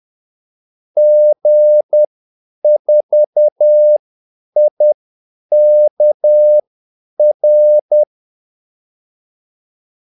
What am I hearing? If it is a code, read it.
Morse code, heard as a keyed tone, about 10 wpm, spaced as for 8 wpm, 600 Hz. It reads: G4IKR
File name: Callsigns-10-8wpm